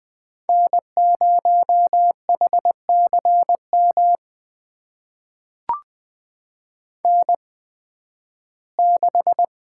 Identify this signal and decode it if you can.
Morse code, heard as a keyed tone, about 20 wpm, spaced as for 13 wpm, 700 Hz. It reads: N0HCM N6
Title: USA Call Signs - 5 Characters - 8x Character Spacing - 20wpm